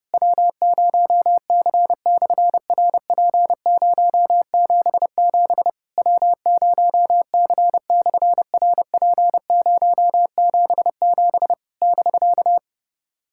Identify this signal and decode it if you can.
Morse code, heard as a keyed tone, about 28 wpm, 700 Hz. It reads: W0C/RP077 W0C/RP077 <BK>
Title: SOTA QSO Practice 30wpm